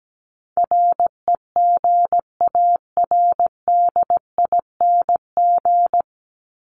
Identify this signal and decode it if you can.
Morse code, heard as a keyed tone, about 17 wpm, 700 Hz. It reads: REGARDING
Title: ING Suffix - Top 100 Words with Base Words 17wpm